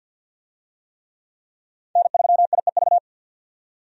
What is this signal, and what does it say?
Morse code, heard as a keyed tone, about 50 wpm, 700 Hz. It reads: N3IEV
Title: USA Call Signs - 5 Characters - 50wpm